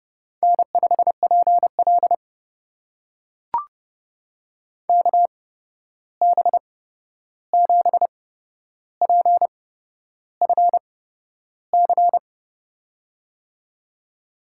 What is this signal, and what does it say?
Morse code, heard as a keyed tone, about 31 wpm, spaced as for 11 wpm, 700 Hz. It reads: N5PL KB7PFC
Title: USA Call Signs - 8x Character Spacing - 30wpm